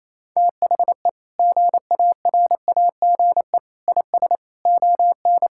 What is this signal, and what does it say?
Morse code, heard as a keyed tone, about 28 wpm, 700 Hz. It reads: THE GARAGE IS ON
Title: Sentences from Top 1600 Words 28wpm